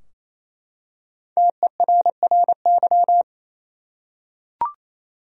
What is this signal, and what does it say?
Morse code, heard as a keyed tone, about 28 wpm, 700 Hz. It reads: TERRY
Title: USA Common Names - Max 5 Letters - 28wpm